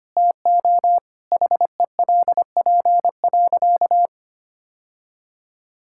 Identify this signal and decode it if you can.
Morse code, heard as a keyed tone, about 25 wpm, 700 Hz. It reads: TO HELP.
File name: practice / 25wpm / Sentences with 3-Word Phrases - 25wpm